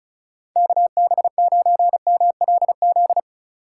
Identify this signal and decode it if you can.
Morse code, heard as a keyed tone, about 35 wpm, 690 Hz. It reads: KB9MLZ
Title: USA Call Signs - 6 Characters - 6x Character Spacing - 35wpm